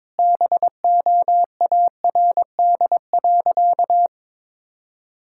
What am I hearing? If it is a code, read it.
Morse code, heard as a keyed tone, about 22 wpm, 700 Hz. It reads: BOARD.
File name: practice / 22wpm / Sentences with 3-Word Phrases - 22wpm